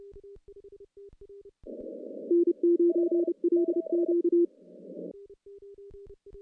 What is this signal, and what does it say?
Morse code, heard as a keyed tone, about 31 wpm, 340 Hz. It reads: N9LQ
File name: CW-SS-k5tr